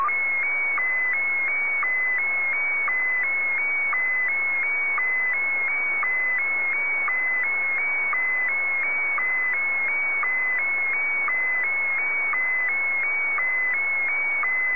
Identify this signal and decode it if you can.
SSTV (slow-scan television), Scottie DX.